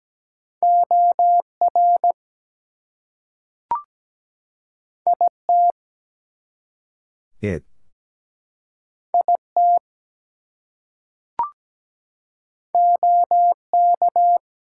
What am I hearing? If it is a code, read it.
Morse code, heard as a keyed tone, about 17 wpm, 700 Hz. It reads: OR IT IT OK
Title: Two Letter Words - 17wpm